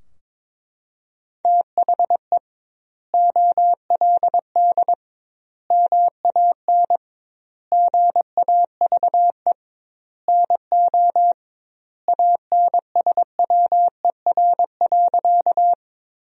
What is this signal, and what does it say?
Morse code, heard as a keyed tone, about 21 wpm, 700 Hz. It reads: THE OLD MAN GAVE NO ANSWER.
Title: 2x Word-Spacing - Sentences from Top 500 Words - 22wpm